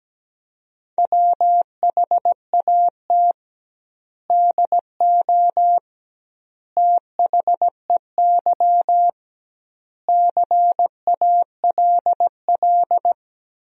Morse code, heard as a keyed tone, about 17 wpm, 700 Hz. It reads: WHAT DO THEY CALL
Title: 2x Word-Spacing - Sentences from Top 100 Words - 17wpm